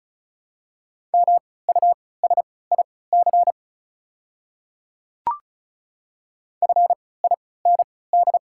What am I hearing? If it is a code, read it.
Morse code, heard as a keyed tone, about 36 wpm, spaced as for 20 wpm, 700 Hz. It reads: MUSIC FIND
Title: 3x Character Spacing - Top 300 Words - 35wpm